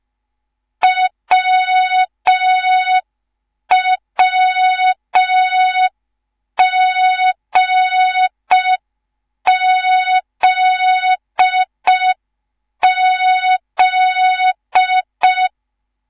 Morse code, heard as a keyed tone, about 5 wpm, 3700 Hz. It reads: WWGZZ